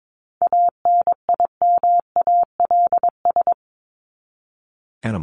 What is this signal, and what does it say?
Morse code, heard as a keyed tone, about 22 wpm, 700 Hz. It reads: ANIMALS
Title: ES:S Suffix - Top 100 Words with Base Words 22wpm